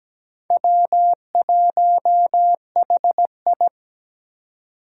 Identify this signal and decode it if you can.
Morse code, heard as a keyed tone, about 17 wpm, 700 Hz. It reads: W1HI
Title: Four Challenging Letter-Number Combinations 17wpm